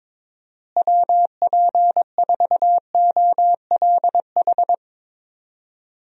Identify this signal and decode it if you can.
Morse code, heard as a keyed tone, about 22 wpm, 700 Hz. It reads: WP4OLH
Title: USA Call Signs - 6 Characters - Repeated 3x - 22wpm